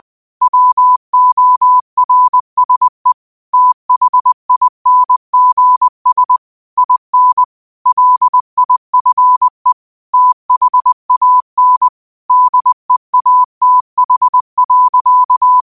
Morse code, heard as a keyed tone, about 20 wpm, 1000 Hz. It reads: WORSE THINGS IN LIFE THAN DEATH.